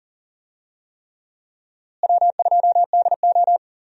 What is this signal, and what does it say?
Morse code, heard as a keyed tone, about 40 wpm, 700 Hz. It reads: W2DO
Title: USA Call Signs - 4 Characters - Repeated 3x - 40wpm